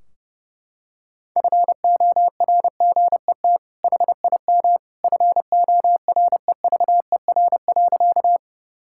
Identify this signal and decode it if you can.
Morse code, heard as a keyed tone, about 32 wpm, 700 Hz. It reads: FORGET HIM FOREVER.
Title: Sentences from Top 1600 Words 30wpm